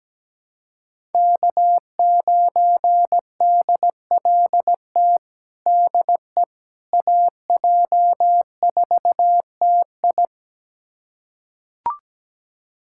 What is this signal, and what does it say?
Morse code, heard as a keyed tone, about 17 wpm, 700 Hz. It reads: K9DLT DE AJ4TI
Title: USA Callsign DE Callsign - 5 Characters - 17wpm